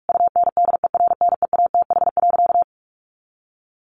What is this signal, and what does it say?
Morse code, heard as a keyed tone, about 44 wpm, 700 Hz. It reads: UNDERNEATH.
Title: A Sound Of Thunder 45wpm